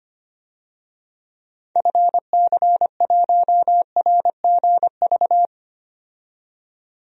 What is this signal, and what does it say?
Morse code, heard as a keyed tone, about 25 wpm, 700 Hz. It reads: FC1RGV
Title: DXCC-100 Call Signs - 6 Characters - 25wpm